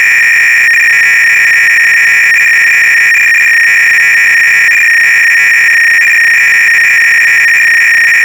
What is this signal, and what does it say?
300-baud modem data.